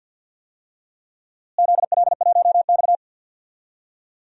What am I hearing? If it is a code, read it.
Morse code, heard as a keyed tone, about 49 wpm, 700 Hz. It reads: ZL1X
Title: DXCC-100 Call Signs - 4 Characters - Repeated 3x - 50wpm